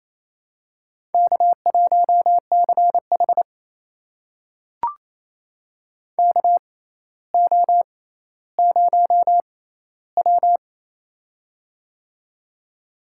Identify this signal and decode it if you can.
Morse code, heard as a keyed tone, about 27 wpm, spaced as for 14 wpm, 700 Hz. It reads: K1CH KO0W
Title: USA Call Signs - 4 Characters - 6x Character Spacing - 28wpm